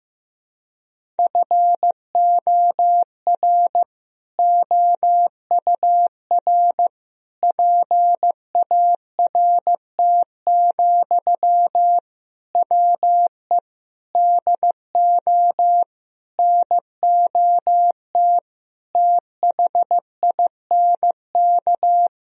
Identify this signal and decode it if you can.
Morse code, heard as a keyed tone, about 15 wpm, 700 Hz. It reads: FOR OUR PART, WE DO NOT THINK
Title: Sentences from Top 1000 Words - Review Part 1 - 15wpm